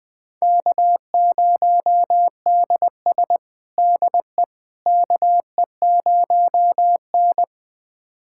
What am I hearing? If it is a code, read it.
Morse code, heard as a keyed tone, about 20 wpm, 700 Hz. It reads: K0DS DE KE0N
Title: USA Callsign DE Callsign - 4 Characters - 20wpm